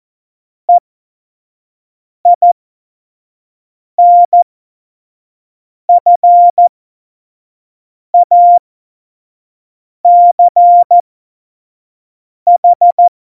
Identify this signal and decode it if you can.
Morse code, heard as a keyed tone, about 14 wpm, spaced as for 5 wpm, 700 Hz. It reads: EINFACH
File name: Morsekurs / Witze / Witz 06. Februar 2018, 25 F70